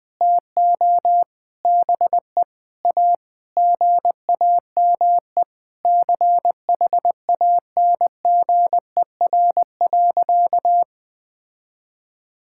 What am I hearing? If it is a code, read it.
Morse code, heard as a keyed tone, about 21 wpm, 700 Hz. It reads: TO BE A GAME CHANGER.
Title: Sentences with 6-Word Phrases - 20wpm